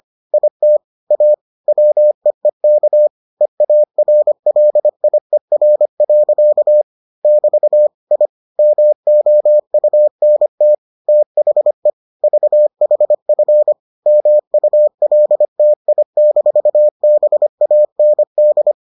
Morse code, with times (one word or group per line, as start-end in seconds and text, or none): 0.3-0.8 IT
1.1-1.3 A
1.7-3.1 WEEK
3.4-6.8 EARLIER.
7.2-7.9 <BT>
8.1-8.3 I
8.6-10.7 MOUNT
11.1-11.9 THE
12.2-13.7 VHF
14.1-18.7 MULTI-BAND